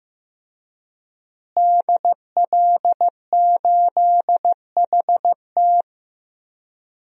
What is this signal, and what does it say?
Morse code, heard as a keyed tone, about 15 wpm, 700 Hz. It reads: DL8HT
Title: DXCC-100 Call Signs - 5 Characters - Repeated 3x - 15wpm